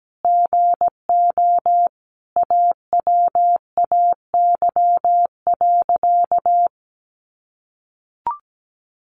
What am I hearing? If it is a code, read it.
Morse code, heard as a keyed tone, about 17 wpm, 700 Hz. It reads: GO AWAY.
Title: Sentences from Top 600 Words 17wpm